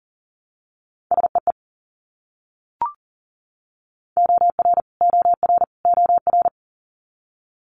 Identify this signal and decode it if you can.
Morse code, heard as a keyed tone, about 37 wpm, 700 Hz. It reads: SEE OR OR OR
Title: Multi-letter Phonemes Repeated 3x 40wpm